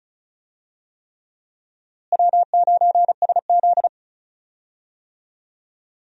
Morse code, heard as a keyed tone, about 36 wpm, 700 Hz. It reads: W9SZ
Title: USA Call Signs - 4 Characters - Repeated 3x - 35wpm